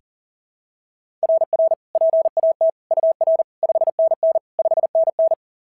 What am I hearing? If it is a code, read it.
Morse code, heard as a keyed tone, about 40 wpm, 690 Hz. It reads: RR PAT UR 5NN 5NN
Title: SOTA QSO Practice 40wpm